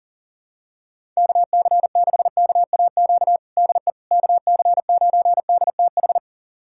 Morse code, heard as a keyed tone, about 40 wpm, 700 Hz. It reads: KC6KAQ DE KC9DTH
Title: USA Callsign DE Callsign - 6 Characters - 40wpm